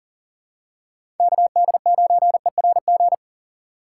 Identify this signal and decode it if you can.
Morse code, heard as a keyed tone, about 37 wpm, 700 Hz. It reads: KD9ERG
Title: USA Call Signs - 6 Characters - 2x Character Spacing - 40wpm